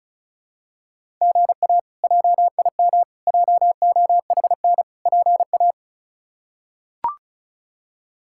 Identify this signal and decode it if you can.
Morse code, heard as a keyed tone, about 34 wpm, 700 Hz. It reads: GA JIM JOHN PA
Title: K1USN SST QSO Practice - 35wpm